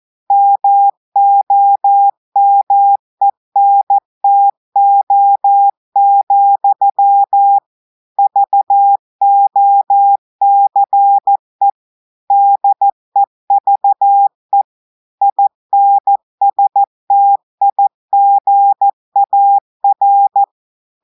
Morse code, heard as a keyed tone, about 14 wpm, 800 Hz. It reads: MOMENTO, VOCE DEVE INSTIGAR
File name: audios / portugais / portugais texte en clair 2 vitesse 14 mots minutes